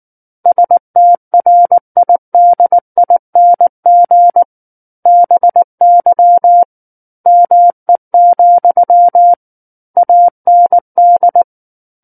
Morse code, heard as a keyed tone, about 19 wpm, 700 Hz. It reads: STRIDING BY ME, AND
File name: Book One / 10-Chapter 10 - 19 WPM